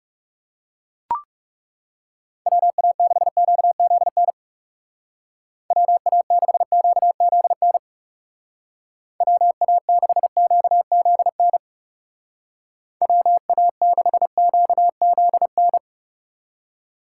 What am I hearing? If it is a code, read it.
Morse code, heard as a keyed tone, about 37 wpm, 700 Hz. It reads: WA6QZN WA6QZN WA6QZN WA6QZN
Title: Speed Racing - USA 2x3 Call Signs 30wpm